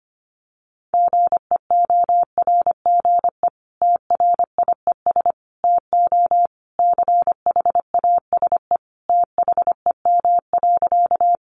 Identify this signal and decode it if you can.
Morse code, heard as a keyed tone, about 25 wpm, 700 Hz. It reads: GEORGE TRIES TO CHASE THEM.
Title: Additional Easy Phrases 25wpm